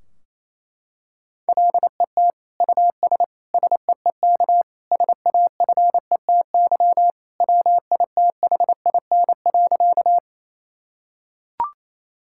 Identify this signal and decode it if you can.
Morse code, heard as a keyed tone, about 28 wpm, 700 Hz. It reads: LET US SEEK SAFETY WITHIN.
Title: Sentences from Top 1600 Words 28wpm